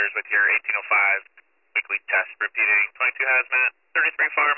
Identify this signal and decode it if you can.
Emergency Alert System (SAME) tones, heard as harsh buzzing bursts, with speech. It is heard from 0.2 to 4.3 s.